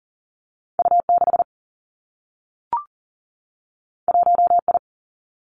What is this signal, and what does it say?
Morse code, heard as a keyed tone, about 40 wpm, 700 Hz. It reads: U 6 1 I
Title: Two Challenging Letter-Number Combinations 40wpm